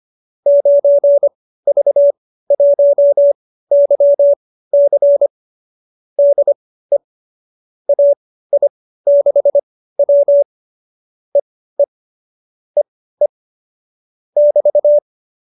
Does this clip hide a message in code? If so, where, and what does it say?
Yes: Morse code from 0.5 to 15.0 s, heard as a keyed tone, about 25 wpm, spaced as for 15 wpm, 570 Hz. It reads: 9V1YC DE AI6W EE EE <BT>